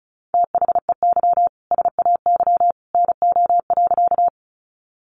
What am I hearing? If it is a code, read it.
Morse code, heard as a keyed tone, about 34 wpm, 700 Hz. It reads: THEY SAY NO.
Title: Easy Phrases 35wpm